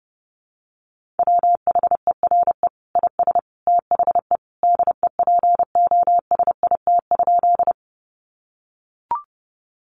Morse code, heard as a keyed tone, about 30 wpm, 700 Hz. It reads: WHERE IS THE DEPOSIT?
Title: Sentences from Top 1300 Words 30wpm